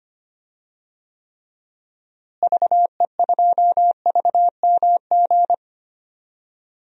Morse code, heard as a keyed tone, about 25 wpm, 700 Hz. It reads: VE2VMG
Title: DXCC-100 Call Signs - 6 Characters - Repeated 3x - 25wpm